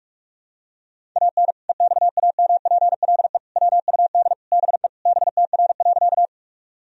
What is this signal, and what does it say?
Morse code, heard as a keyed tone, about 46 wpm, 700 Hz. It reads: AN EXAMPLE WUD BE BTR.
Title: Sentences from Top 400 Words and Abbreviations 45wpm